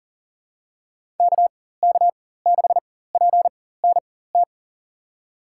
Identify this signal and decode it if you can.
Morse code, heard as a keyed tone, about 38 wpm, spaced as for 18 wpm, 700 Hz. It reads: KK6PNT
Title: USA Call Signs - 6 Characters - 4x Character Spacing - 40wpm